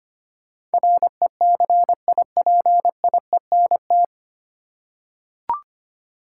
Morse code, heard as a keyed tone, about 25 wpm, 690 Hz. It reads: RECIPIENT
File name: practice / 25wpm / 9 Letter Words - First 3 Letters Given - 25wpm